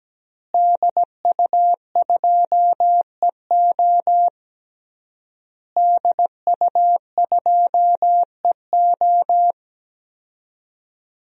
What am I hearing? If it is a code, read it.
Morse code, heard as a keyed tone, about 17 wpm, 700 Hz. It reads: DU2EO DU2EO
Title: DXCC-100 Call Signs - 5 Characters - Repeated 3x - 17wpm